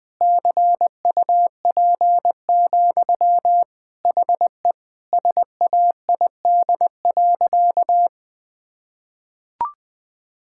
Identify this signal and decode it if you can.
Morse code, heard as a keyed tone, about 20 wpm, 700 Hz. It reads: CUP, HE SAID.
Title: Sentences from Top 1600 Words 20wpm